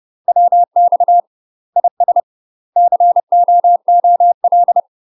Morse code, heard as a keyed tone, about 30 wpm, 700 Hz. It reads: WX IS COOL
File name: UK MASTER QSOs TRAINING/30WPM